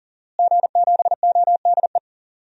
Morse code, heard as a keyed tone, about 40 wpm, 700 Hz. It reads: G7ODE
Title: DXCC-100 Call Signs - 5 Characters - Repeated 2x - 40wpm